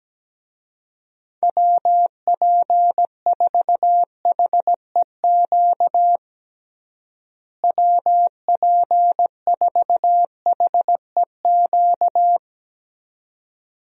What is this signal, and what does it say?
Morse code, heard as a keyed tone, about 17 wpm, 700 Hz. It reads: WP4HEQ WP4HEQ
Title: USA Call Signs - 6 Characters - Repeated 2x - 17wpm